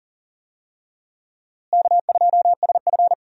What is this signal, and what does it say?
Morse code, heard as a keyed tone, about 40 wpm, 700 Hz. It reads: K2SF
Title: USA Call Signs - 4 Characters - Repeated 3x - 40wpm